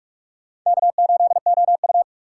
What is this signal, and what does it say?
Morse code, heard as a keyed tone, about 46 wpm, 700 Hz. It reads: K8OU
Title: USA Call Signs - 4 Characters - 6x Character Spacing - 45wpm